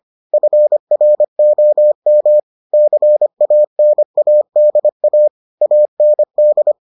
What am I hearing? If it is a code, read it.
Morse code, heard as a keyed tone, about 25 wpm, 600 Hz. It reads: FROM CANADA AND